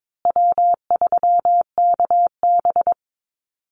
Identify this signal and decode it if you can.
Morse code, heard as a keyed tone, about 22 wpm, 700 Hz. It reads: W3KB
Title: Easy Call Signs Repeated 2x - 22wpm